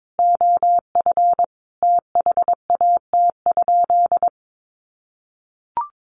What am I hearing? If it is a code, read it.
Morse code, heard as a keyed tone, about 22 wpm, 700 Hz. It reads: OF THAT?
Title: Sentences from Top 1200 Words 22wpm